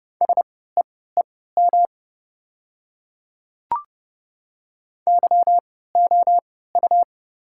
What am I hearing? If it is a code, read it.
Morse code, heard as a keyed tone, about 30 wpm, spaced as for 17 wpm, 700 Hz. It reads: SEEM YOU
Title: 3x Character Spacing - Top 300 Words - 30wpm